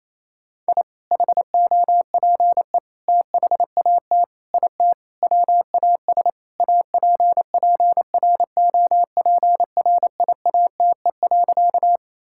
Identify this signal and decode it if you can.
Morse code, heard as a keyed tone, about 28 wpm, 700 Hz. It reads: I HOPE THAT IT WAS APPROPRIATE.
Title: Sentences from Top 1000 Words - Review Part 5 - 28wpm